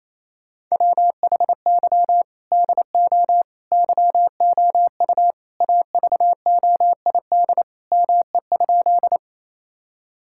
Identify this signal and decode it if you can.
Morse code, heard as a keyed tone, about 28 wpm, 700 Hz. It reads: WHY DO YOU AVOID ME?
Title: Sentences from Top 1000 Words - Review Part 6 - 28wpm